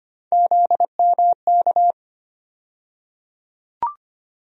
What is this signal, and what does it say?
Morse code, heard as a keyed tone, about 25 wpm, 700 Hz. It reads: ZMK
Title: USA Call Sign Postfixes - 3 Letters 25wpm